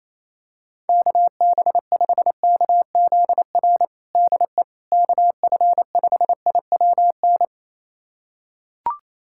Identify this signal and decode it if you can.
Morse code, heard as a keyed tone, about 28 wpm, 700 Hz. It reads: KB5KZR DE KF5IWN
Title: USA Callsign DE Callsign - 6 Characters - 28wpm